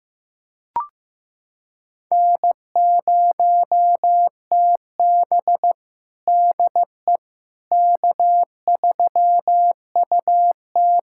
Morse code, heard as a keyed tone, about 15 wpm, 700 Hz. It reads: N0TB DE K3UT
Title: USA Callsign DE Callsign - 4 Characters - 15wpm